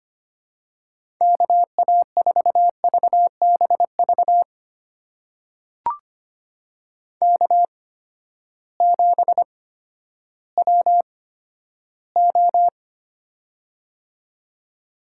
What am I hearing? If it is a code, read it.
Morse code, heard as a keyed tone, about 25 wpm, spaced as for 12 wpm, 700 Hz. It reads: KA4VBV K7WO
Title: USA Call Signs - 8x Character Spacing - 25wpm